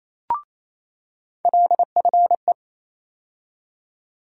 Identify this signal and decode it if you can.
Morse code, heard as a keyed tone, about 28 wpm, 700 Hz. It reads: LFE
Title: USA Call Sign Postfixes - 3 Letters 28wpm